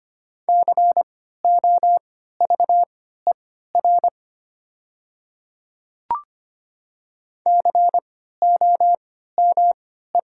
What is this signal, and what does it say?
Morse code, heard as a keyed tone, about 25 wpm, spaced as for 14 wpm, 700 Hz. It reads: COVER COME
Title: 3x Character Spacing - Top 300 Words - 25wpm